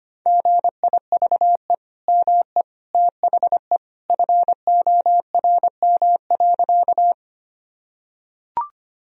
Morse code, heard as a keyed tone, about 25 wpm, 700 Hz. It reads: GIVE ME THE FORM.
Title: Sentences from Top 1000 Words - Review Part 1 - 25wpm